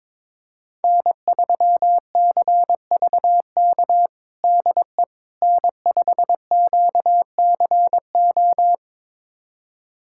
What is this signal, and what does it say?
Morse code, heard as a keyed tone, about 22 wpm, 700 Hz. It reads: N3CVK DE N5QCO
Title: USA Callsign DE Callsign - 5 Characters - 22wpm